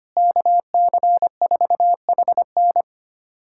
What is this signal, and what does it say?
Morse code, heard as a keyed tone, about 25 wpm, 700 Hz. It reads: KC4HN
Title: USA Callsign DE Callsign - 5 Characters - 25wpm